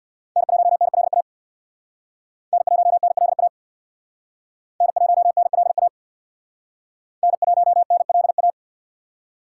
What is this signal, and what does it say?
Morse code, heard as a keyed tone, about 61 wpm, 700 Hz. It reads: N1NLA N1NLA N1NLA N1NLA
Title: Speed Racing - USA 1x3 Call Signs 50wpm